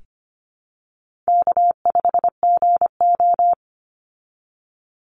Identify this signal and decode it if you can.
Morse code, heard as a keyed tone, about 25 wpm, 700 Hz. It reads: K5GO
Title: Call Signs Repeated 2x - 25wpm